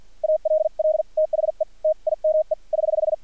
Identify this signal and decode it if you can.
Morse code, heard as a keyed tone, about 50 wpm, spaced as for 43 wpm, 620 Hz. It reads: ALL THE TIME <HH>
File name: archived / CRSnet